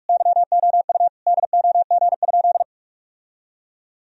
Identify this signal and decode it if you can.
Morse code, heard as a keyed tone, about 41 wpm, 700 Hz. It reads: YOU DOG?
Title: Sentences from Top 1000 Words - Review Part 2 - 45wpm